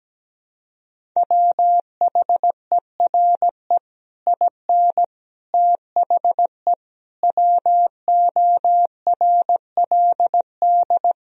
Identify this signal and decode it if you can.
Morse code, heard as a keyed tone, about 17 wpm, 700 Hz. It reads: WHERE IN THE WORLD
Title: Sentences from Top 200 Words and Abbreviations 17wpm